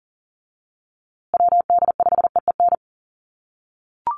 Morse code, heard as a keyed tone, about 39 wpm, 700 Hz. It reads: WD5EEN
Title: USA Call Signs Repeated 3x 40wpm